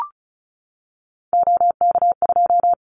Morse code, heard as a keyed tone, about 36 wpm, 700 Hz. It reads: OK2
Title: Callsign Prefixes for Top 50 DXCC - 35wpm